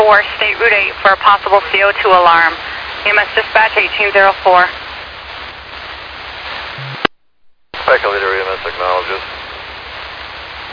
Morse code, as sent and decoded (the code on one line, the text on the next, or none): none